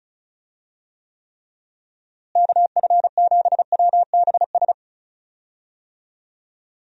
Morse code, heard as a keyed tone, about 36 wpm, 700 Hz. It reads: KF7WBS
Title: USA Call Signs - 6 Characters - Repeated 3x - 35wpm